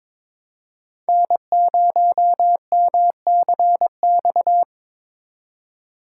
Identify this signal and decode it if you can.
Morse code, heard as a keyed tone, about 21 wpm, 700 Hz. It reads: N0MCX
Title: USA Call Signs - 5 Characters - 4x Character Spacing - 22wpm